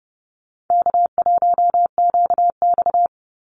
Morse code, heard as a keyed tone, about 30 wpm, 700 Hz. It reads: K1QX
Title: Easy Call Signs Repeated 3x - 30wpm